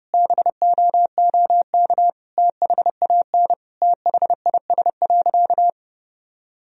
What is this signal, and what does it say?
Morse code, heard as a keyed tone, about 30 wpm, 700 Hz. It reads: BOOK THAN THIS.
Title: Sentences from Top 1000 Words - Review Part 3 - 30wpm